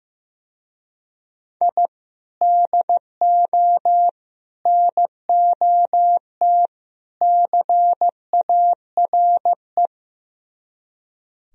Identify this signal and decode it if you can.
Morse code, heard as a keyed tone, about 15 wpm, 700 Hz. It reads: I DO NOT CARE
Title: Speed-Racing - Sets of 4 Words 15wpm